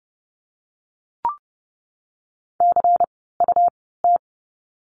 Morse code, heard as a keyed tone, about 30 wpm, spaced as for 17 wpm, 700 Hz. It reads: CUT
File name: Lesson 17f - Letter u - Words 30wpm